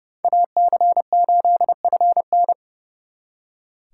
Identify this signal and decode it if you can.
Morse code, heard as a keyed tone, about 37 wpm, 700 Hz. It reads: AC8FN
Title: Speed Racing - USA 2x2 Call Signs 30wpm